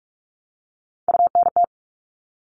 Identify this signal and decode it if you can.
Morse code, heard as a keyed tone, about 44 wpm, 700 Hz. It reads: UNT
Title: All Cut Numbers - 3-Digit 45wpm